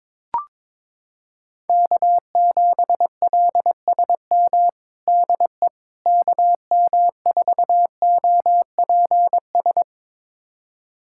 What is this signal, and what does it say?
Morse code, heard as a keyed tone, about 22 wpm, 700 Hz. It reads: K7LSM DE KM4OPS
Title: USA Callsign DE Callsign 22wpm